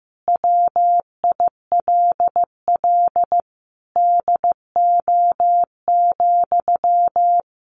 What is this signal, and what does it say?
Morse code, heard as a keyed tone, about 15 wpm, 700 Hz. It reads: WILL DO,
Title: Sentences from Top 100 Words 15wpm